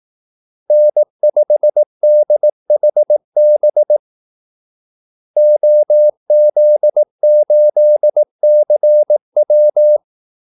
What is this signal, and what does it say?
Morse code, heard as a keyed tone, about 18 wpm, 600 Hz. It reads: N5DHB OZ8CW
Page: Kaldesignaler 18 wpm